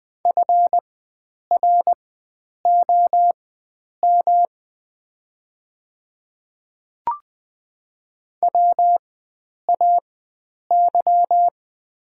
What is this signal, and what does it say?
Morse code, heard as a keyed tone, about 20 wpm, spaced as for 9 wpm, 700 Hz. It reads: FROM WAY
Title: 4x Character Spacing - Top 100 Words - 20wpm